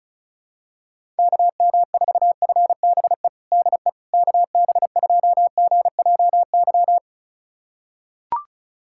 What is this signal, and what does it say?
Morse code, heard as a keyed tone, about 34 wpm, 700 Hz. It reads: KM4FBE DE KB2GJY
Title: USA Callsign DE Callsign - 6 Characters - 35wpm